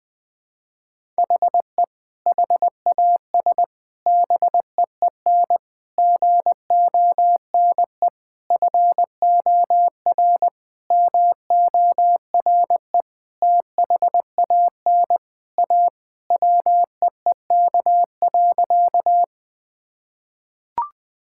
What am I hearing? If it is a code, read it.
Morse code, heard as a keyed tone, about 21 wpm, 700 Hz. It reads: HE HAS BEEN GONE FOR MORE THAN A WEEK.
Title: Sentences with 4-Word Phrases - 20wpm